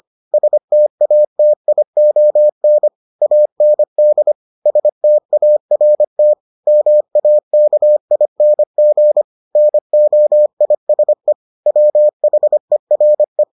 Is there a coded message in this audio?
Yes: Morse code from 0.3 to 13.4 s, heard as a keyed tone, about 25 wpm, 600 Hz. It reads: STATION AND START MAKING NOISE WHERE